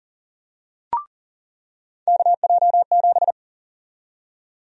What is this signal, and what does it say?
Morse code, heard as a keyed tone, about 40 wpm, 700 Hz. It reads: KJ7
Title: USA Call Sign Prefixes - 40wpm